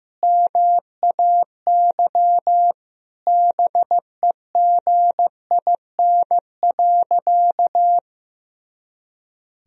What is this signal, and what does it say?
Morse code, heard as a keyed tone, about 15 wpm, 700 Hz. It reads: MAY BEGIN.
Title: Sentences from Top 300 Words and Abbreviations 15wpm